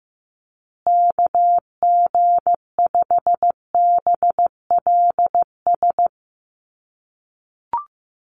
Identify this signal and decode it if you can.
Morse code, heard as a keyed tone, about 14 wpm, 700 Hz. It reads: KG5BLS
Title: USA Call Signs Repeated 2x 15wpm